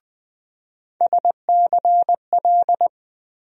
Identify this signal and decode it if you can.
Morse code, heard as a keyed tone, about 20 wpm, 700 Hz. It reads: SCL
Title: USA Call Sign Postfixes - 3 Letters 20wpm